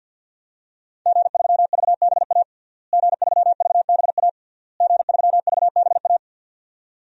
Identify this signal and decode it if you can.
Morse code, heard as a keyed tone, about 52 wpm, 700 Hz. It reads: G3VBA G3VBA G3VBA
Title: DXCC-100 Call Signs - 5 Characters - Repeated 3x - 50wpm